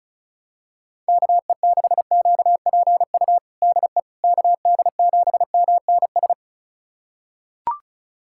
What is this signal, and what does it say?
Morse code, heard as a keyed tone, about 36 wpm, 700 Hz. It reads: KE6QPU DE KD7MNS
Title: USA Callsign DE Callsign 35wpm